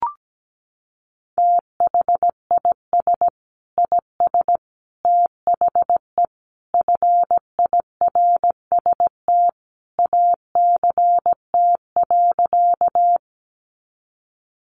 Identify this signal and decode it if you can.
Morse code, heard as a keyed tone, about 17 wpm, 700 Hz. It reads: THIS IS THE FIRST ACT.
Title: Sentences from Top 200 Words 17wpm